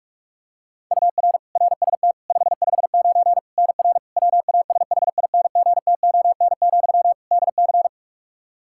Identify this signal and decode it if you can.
Morse code, heard as a keyed tone, about 46 wpm, 700 Hz. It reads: UR RST 559 NR WASHINGTON, DC
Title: K1USN SST QSO Practice 45wpm